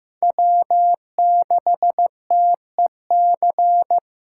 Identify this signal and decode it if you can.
Morse code, heard as a keyed tone, about 15 wpm, 700 Hz. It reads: W6TEC
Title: USA Call Signs - 5 Characters - 2x Character Spacing - 15wpm